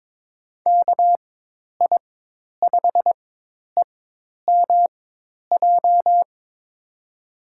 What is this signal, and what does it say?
Morse code, heard as a keyed tone, about 22 wpm, spaced as for 10 wpm, 700 Hz. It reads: KI5EMJ
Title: USA Call Signs - 6 Characters - 4x Character Spacing - 22wpm